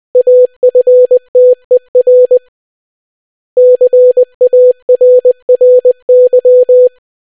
Morse code, heard as a keyed tone, about 20 wpm, 500 Hz. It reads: AFTER CARRY